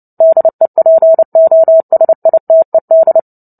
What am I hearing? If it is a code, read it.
Morse code, heard as a keyed tone, about 30 wpm, 650 Hz. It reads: DEPOSITED
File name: Book Two / 03-Chapter 3 - 29 WPM